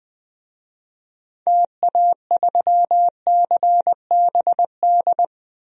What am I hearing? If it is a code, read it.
Morse code, heard as a keyed tone, about 20 wpm, 700 Hz. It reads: TA3CBD
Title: DXCC-100 Call Signs - 6 Characters - Repeated 3x - 20wpm